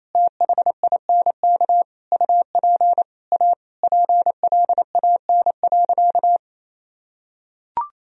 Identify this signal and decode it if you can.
Morse code, heard as a keyed tone, about 28 wpm, 700 Hz. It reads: THINK UP A PLAN.
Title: Sentences from Top 1000 Words - Review Part 2 - 28wpm